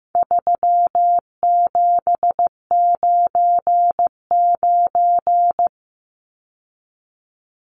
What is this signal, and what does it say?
Morse code, heard as a keyed tone, about 15 wpm, 700 Hz. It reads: 3799
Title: Numbers - 4-Digit 15wpm